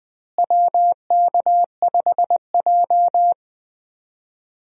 Morse code, heard as a keyed tone, about 20 wpm, 700 Hz. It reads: WK5J
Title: USA Call Signs - 4 Characters - Repeated 3x - 20wpm